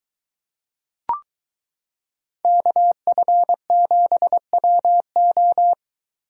Morse code, heard as a keyed tone, about 23 wpm, 700 Hz. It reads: KF7WO